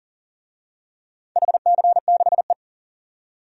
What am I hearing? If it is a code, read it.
Morse code, heard as a keyed tone, about 40 wpm, 700 Hz. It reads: HC6E
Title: DXCC-100 Call Signs - 4 Characters - 40wpm